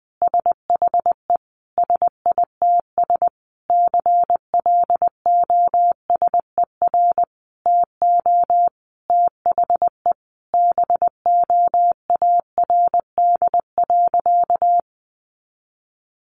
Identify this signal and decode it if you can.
Morse code, heard as a keyed tone, about 20 wpm, 700 Hz. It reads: SHE SITS CLOSER TO THE BOARD.
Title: Additional Easy Phrases 20wpm